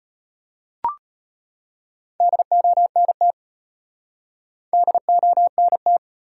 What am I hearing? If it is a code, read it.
Morse code, heard as a keyed tone, about 33 wpm, 700 Hz. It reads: DONT DONT